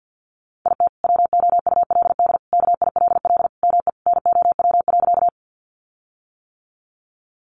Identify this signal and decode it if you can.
Morse code, heard as a keyed tone, about 48 wpm, 700 Hz. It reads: IT WOULD KILL ME NOW.
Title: Sentences from Top 1400 Words 50wpm